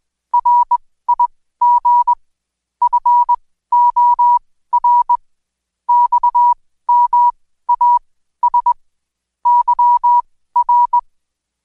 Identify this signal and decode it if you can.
Morse code, heard as a keyed tone, about 21 wpm, spaced as for 14 wpm, 970 Hz. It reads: RIG FOR XMAS YR
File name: Code Quick / AudioDownld2 / Seg123-Day28-DayDrill 28